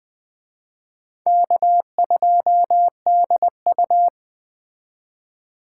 Morse code, heard as a keyed tone, about 21 wpm, 700 Hz. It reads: K2DU